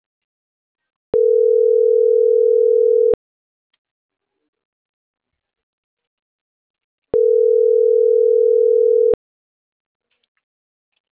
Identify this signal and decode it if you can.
Ringback tone (North America), 440 + 480 Hz.